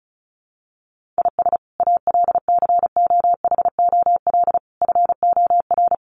Morse code, heard as a keyed tone, about 34 wpm, 700 Hz. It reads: IS ALCOHOL FOR
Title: Sentences from Top 1400 Words 35wpm